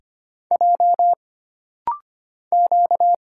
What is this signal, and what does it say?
Morse code, heard as a keyed tone, about 25 wpm, 700 Hz. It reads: J Q
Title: Single Letters - Rapid-Fire 25wpm